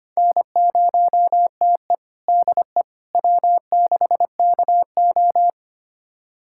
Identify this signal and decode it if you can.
Morse code, heard as a keyed tone, about 25 wpm, 700 Hz. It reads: N0TE DE W6KO
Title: USA Callsign DE Callsign - 4 Characters - 25wpm